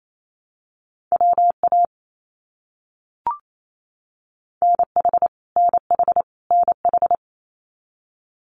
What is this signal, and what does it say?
Morse code, heard as a keyed tone, about 28 wpm, 700 Hz. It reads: WA NH NH NH
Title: US States Repeated 3x 28wpm